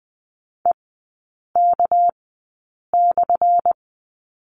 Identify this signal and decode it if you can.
Morse code, heard as a keyed tone, about 20 wpm, spaced as for 8 wpm, 700 Hz. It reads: EK/
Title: 198 Lesson 32e - Letter k - Three Characters 20wpm